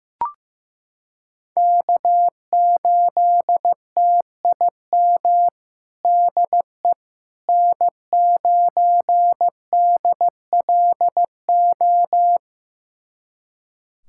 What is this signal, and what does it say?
Morse code, heard as a keyed tone, about 15 wpm, 700 Hz. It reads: K8TIM DE N9DLO
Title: USA Callsign DE Callsign - 5 Characters - 15wpm